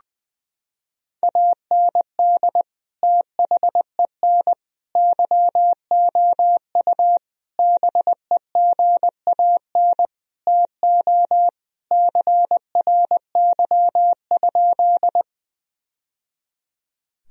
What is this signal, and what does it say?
Morse code, heard as a keyed tone, about 20 wpm, 700 Hz. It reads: AND THEN YOU BEGAN TO CRY?
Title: Sentences from Top 1000 Words - Review Part 2 - 20wpm